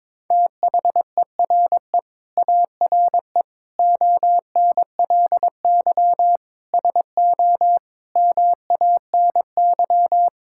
Morse code, heard as a keyed tone, about 22 wpm, 700 Hz. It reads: THERE ARE ONLY SO MANY
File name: Sentences with 9-Word Phrases - 22wpm